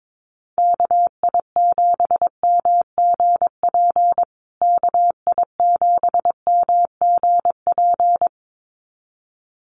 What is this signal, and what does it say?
Morse code, heard as a keyed tone, about 22 wpm, 700 Hz. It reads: KI7MGP KI7MGP
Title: USA Call Signs Repeated 2x 22wpm